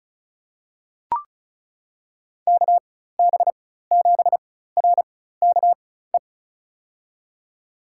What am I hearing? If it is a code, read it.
Morse code, heard as a keyed tone, about 34 wpm, spaced as for 16 wpm, 700 Hz. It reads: KB7RKE
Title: USA Call Signs - 6 Characters - 4x Character Spacing - 35wpm